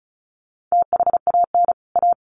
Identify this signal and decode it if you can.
Morse code, heard as a keyed tone, about 34 wpm, 700 Hz. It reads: THAN A
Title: Sets of 2 Words 35wpm